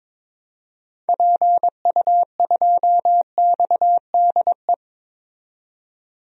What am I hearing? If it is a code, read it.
Morse code, heard as a keyed tone, about 22 wpm, 700 Hz. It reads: PU2XDE
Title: DXCC-100 Call Signs - 6 Characters - 22wpm